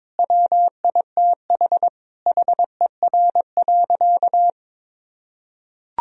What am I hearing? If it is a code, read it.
Morse code, heard as a keyed tone, about 22 wpm, 700 Hz. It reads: WITH HER.
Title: Sentences from Top 1000 Words - Review Part 3 - 22wpm